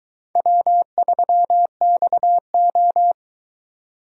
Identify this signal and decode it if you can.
Morse code, heard as a keyed tone, about 22 wpm, 700 Hz. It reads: W3XO